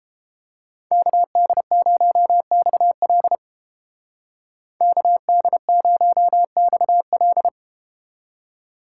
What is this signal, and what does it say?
Morse code, heard as a keyed tone, about 31 wpm, 700 Hz. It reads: KD0XL KD0XL